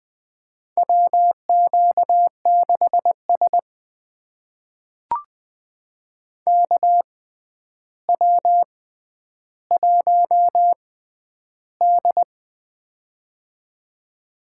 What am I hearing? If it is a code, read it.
Morse code, heard as a keyed tone, about 20 wpm, spaced as for 10 wpm, 700 Hz. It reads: WQ6S KW1D
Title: USA Call Signs - 4 Characters - 6x Character Spacing - 20wpm